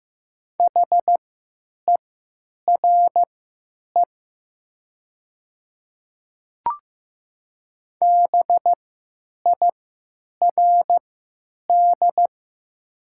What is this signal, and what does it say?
Morse code, heard as a keyed tone, about 15 wpm, spaced as for 9 wpm, 700 Hz. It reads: HERE BIRD
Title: 3x Character Spacing - Top 500 Words - 15wpm